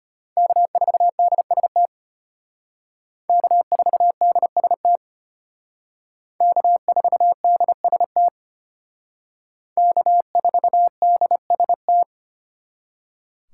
Morse code, heard as a keyed tone, about 32 wpm, 700 Hz. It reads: K4DST K4DST K4DST K4DST